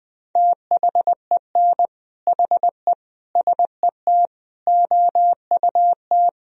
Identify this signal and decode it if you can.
Morse code, heard as a keyed tone, about 20 wpm, 700 Hz. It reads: THEN HE SET OUT
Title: Sentences from Top 1000 Words - Review Part 1 - 20wpm